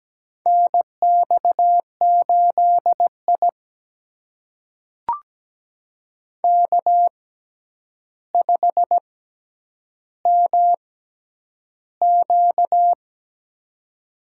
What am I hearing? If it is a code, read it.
Morse code, heard as a keyed tone, about 17 wpm, spaced as for 9 wpm, 700 Hz. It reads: NX8I K5MQ
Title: USA Call Signs - 4 Characters - 6x Character Spacing - 17wpm